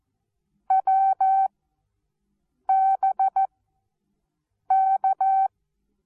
Morse code, heard as a keyed tone, about 14 wpm, 770 Hz. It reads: W B K
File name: Onlinelehrgang / morsen / DJ4UF Morsen19